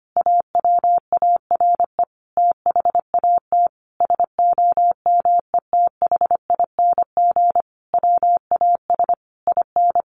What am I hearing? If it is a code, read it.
Morse code, heard as a keyed tone, about 25 wpm, 700 Hz. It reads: AWARE THAT SOMETHING WAS IN
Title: Sentences from Top 1400 Words 25wpm